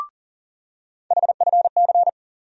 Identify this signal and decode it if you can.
Morse code, heard as a keyed tone, about 40 wpm, 700 Hz. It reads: HFC
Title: USA Call Sign Postfixes - 3 Letters 40wpm